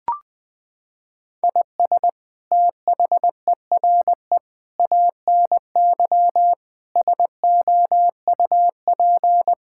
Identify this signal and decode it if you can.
Morse code, heard as a keyed tone, about 21 wpm, 700 Hz. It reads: IS THERE ANY SOUP
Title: Sentences from Top 1000 Words - Review Part 6 - 20wpm